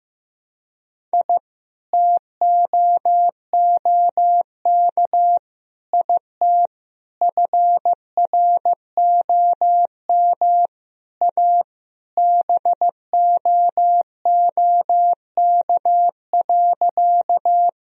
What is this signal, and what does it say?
Morse code, heard as a keyed tone, about 15 wpm, 700 Hz. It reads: I TOOK IT FROM A BOOK.
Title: Sentences from Top 1000 Words - Review Part 1 - 15wpm